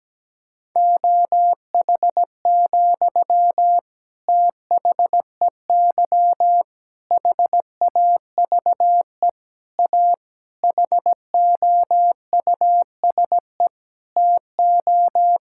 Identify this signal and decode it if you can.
Morse code, heard as a keyed tone, about 17 wpm, 700 Hz. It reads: OH, THEY HAVE A HOUSE TO
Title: Sentences from Top 1000 Words - Review Part 5 - 17wpm